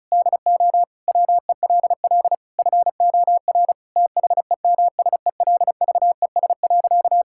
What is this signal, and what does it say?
Morse code, heard as a keyed tone, about 36 wpm, 700 Hz. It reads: DO WELL FOR THEMSELVES.
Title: Sentences from Top 1000 Words - Review Part 5 - 35wpm